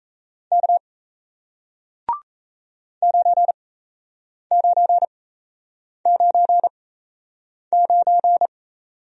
Morse code, heard as a keyed tone, about 38 wpm, spaced as for 8 wpm, 700 Hz. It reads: K 9999